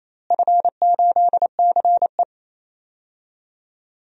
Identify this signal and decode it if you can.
Morse code, heard as a keyed tone, about 27 wpm, 700 Hz. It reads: F8CE
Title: DXCC-100 Call Signs - 4 Characters - Repeated 3x - 28wpm